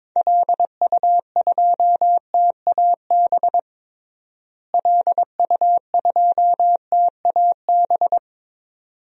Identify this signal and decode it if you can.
Morse code, heard as a keyed tone, about 22 wpm, 700 Hz. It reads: LU2TAB LU2TAB
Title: DXCC-100 Call Signs - 6 Characters - Repeated 2x - 22wpm